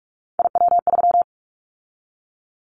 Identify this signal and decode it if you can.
Morse code, heard as a keyed tone, about 46 wpm, 700 Hz. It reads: IW3
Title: Callsign Prefixes for Top 50 DXCC - 45wpm